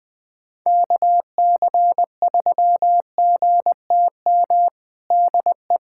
Morse code, heard as a keyed tone, about 20 wpm, 700 Hz. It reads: KC3GTM DE
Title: USA Callsign DE Callsign - 6 Characters - 20wpm